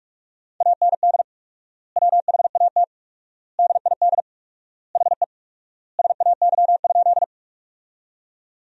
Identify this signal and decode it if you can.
Morse code, heard as a keyed tone, about 46 wpm, 700 Hz. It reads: AND WHAT DID HE SAY?
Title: 4x Word-Spacing - Sentences from Top 200 Words - 45wpm